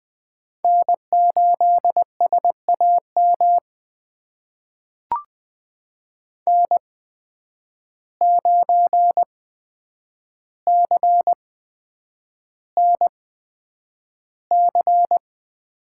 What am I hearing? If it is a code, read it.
Morse code, heard as a keyed tone, about 20 wpm, spaced as for 9 wpm, 700 Hz. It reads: N8SAM N9CNC
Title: USA Call Signs - 5 Characters - 8x Character Spacing - 20wpm